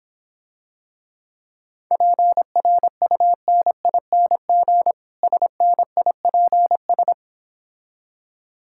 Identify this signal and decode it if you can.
Morse code, heard as a keyed tone, about 26 wpm, 700 Hz. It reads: PRUNING SNIPS